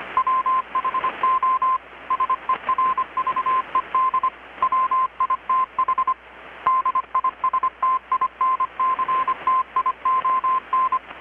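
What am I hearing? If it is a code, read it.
Morse code, heard as a keyed tone, about 25 wpm, 1040 Hz. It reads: WHO SERVED WITH DISTINCTION
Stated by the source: Morse from KPH's Night of Nights XX maritime radiotelegraphy event, received in Chiapas, Mexico about 3,800 km away on a Kenwood R-600 with a grounded 10 m long-wire antenna.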